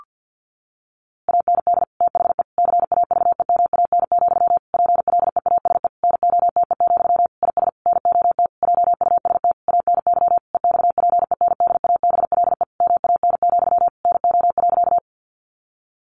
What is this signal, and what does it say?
Morse code, heard as a keyed tone, about 48 wpm, 700 Hz. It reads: AND THE CAVEMAN, PLEASE NOTE, IS NOT JUST ANY EXPENDABLE MAN, NO.
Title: A Sound Of Thunder 50wpm